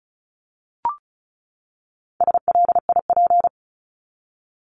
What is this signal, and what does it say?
Morse code, heard as a keyed tone, about 34 wpm, 700 Hz. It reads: SLIP
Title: Top 1751-2000 Words 35wpm